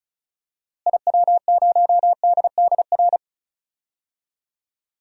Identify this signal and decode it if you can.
Morse code, heard as a keyed tone, about 36 wpm, 700 Hz. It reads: IW0DDR
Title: DXCC-100 Call Signs - 6 Characters - Repeated 3x - 35wpm